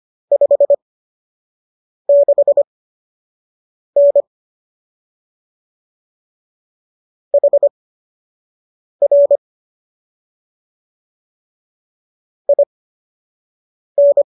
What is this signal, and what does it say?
Morse code, heard as a keyed tone, about 25 wpm, spaced as for 6 wpm, 570 Hz. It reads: 56N HR IN